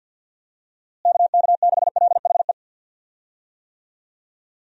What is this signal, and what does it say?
Morse code, heard as a keyed tone, about 51 wpm, 700 Hz. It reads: KK6LHE
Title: USA Call Signs - 6 Characters - Repeated 2x - 50wpm